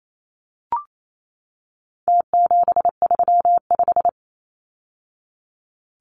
Morse code, heard as a keyed tone, about 28 wpm, 700 Hz. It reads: T735
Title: Numbers - Contest Style - 4-Digit 28wpm